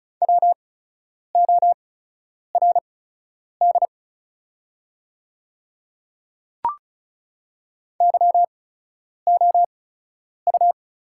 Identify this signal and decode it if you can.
Morse code, heard as a keyed tone, about 36 wpm, spaced as for 10 wpm, 700 Hz. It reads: WORD YOU
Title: 8x Character Spacing - Top 100 Words - 35wpm